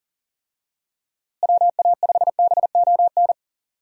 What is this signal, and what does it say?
Morse code, heard as a keyed tone, about 40 wpm, 700 Hz. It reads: WA5BON
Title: USA Call Signs - 6 Characters - Repeated 2x - 40wpm